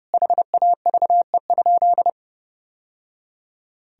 Morse code, heard as a keyed tone, about 30 wpm, 700 Hz. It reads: HAVE?
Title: Sentences from Top 1000 Words - Review Part 3 - 30wpm